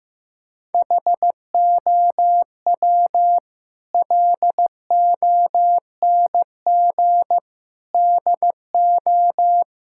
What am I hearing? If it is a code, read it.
Morse code, heard as a keyed tone, about 15 wpm, 700 Hz. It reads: HOW LONG DO
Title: Sentences from Top 1600 Words 15wpm